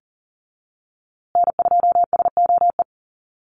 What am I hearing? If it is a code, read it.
Morse code, heard as a keyed tone, about 40 wpm, 700 Hz. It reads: N2SOE
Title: USA Call Signs Repeated 2x 40wpm